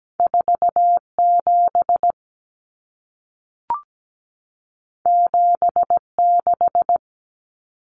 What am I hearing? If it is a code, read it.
Morse code, heard as a keyed tone, about 17 wpm, 700 Hz. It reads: 47 76
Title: Numbers - 2-Digit 17wpm